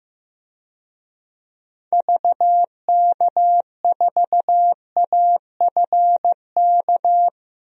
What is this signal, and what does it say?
Morse code, heard as a keyed tone, about 15 wpm, 700 Hz. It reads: VK4AFK
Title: DXCC-100 Call Signs - 6 Characters - Repeated 3x - 15wpm